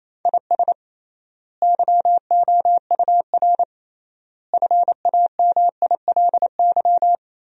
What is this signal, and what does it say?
Morse code, heard as a keyed tone, about 28 wpm, 700 Hz. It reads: IS YOUR FAMILY
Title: 3x Word-Spacing - Sentences from Top 400 Words - 28wpm